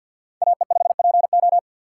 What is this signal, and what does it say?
Morse code, heard as a keyed tone, about 61 wpm, 690 Hz. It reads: AE5PO
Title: Speed Racing - USA 2x2 Call Signs 50wpm